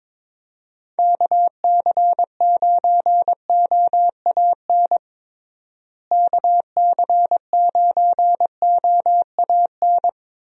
Morse code, heard as a keyed tone, about 22 wpm, 700 Hz. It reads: KC9OAN KC9OAN
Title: USA Call Signs - 6 Characters - Repeated 3x - 22wpm